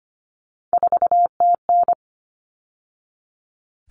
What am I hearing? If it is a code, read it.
Morse code, heard as a keyed tone, about 25 wpm, 700 Hz. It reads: 4TN
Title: All Cut Numbers - 3-Digit 25wpm